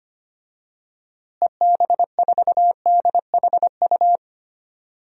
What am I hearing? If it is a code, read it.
Morse code, heard as a keyed tone, about 25 wpm, 700 Hz. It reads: EB4DHU
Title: DXCC-100 Call Signs - 6 Characters - Repeated 2x - 25wpm